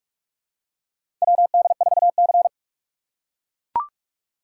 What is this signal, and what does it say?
Morse code, heard as a keyed tone, about 43 wpm, 700 Hz. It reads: WD4C
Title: USA Call Signs - 2x Character Spacing - 45wpm